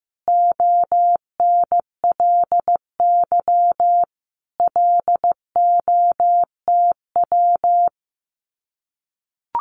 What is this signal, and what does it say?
Morse code, heard as a keyed tone, about 15 wpm, 700 Hz. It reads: ONLY LOTW
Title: Common QSO Phrases Encore 15wpm